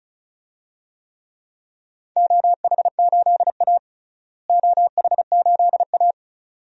Morse code, heard as a keyed tone, about 34 wpm, 700 Hz. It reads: OH8A OH8A
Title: DXCC-100 Call Signs - 4 Characters - Repeated 2x - 35wpm